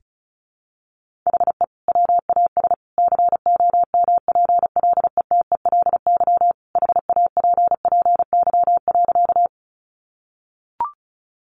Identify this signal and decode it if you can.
Morse code, heard as a keyed tone, about 34 wpm, 700 Hz. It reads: HE WAS COMPLETELY HAPPY.
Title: Sentences from Top 1100 Words 35wpm